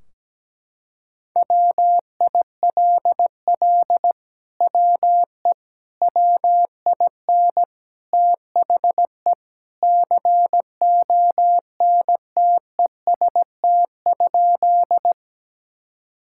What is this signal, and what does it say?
Morse code, heard as a keyed tone, about 17 wpm, 700 Hz. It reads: WILL WE WIN THE CONTEST?
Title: Sentences from Top 1600 Words 17wpm